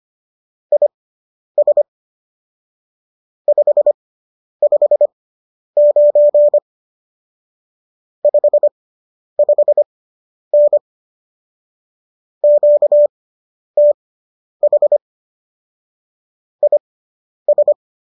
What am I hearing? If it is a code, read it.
Morse code, heard as a keyed tone, about 25 wpm, spaced as for 10 wpm, 590 Hz. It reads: IS 559 55N QTH IS